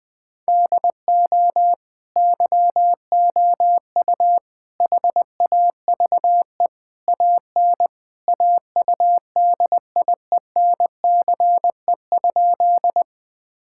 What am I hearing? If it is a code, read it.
Morse code, heard as a keyed tone, about 20 wpm, 700 Hz. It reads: DO YOU HAVE AN AUDIENCE?
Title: Sentences from Top 1000 Words - Review Part 5 - 20wpm